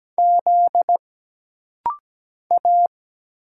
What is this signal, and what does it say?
Morse code, heard as a keyed tone, about 17 wpm, 700 Hz. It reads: Z A
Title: Single Letters - Rapid-Fire 17wpm